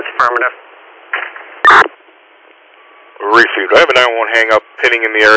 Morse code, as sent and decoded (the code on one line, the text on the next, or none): none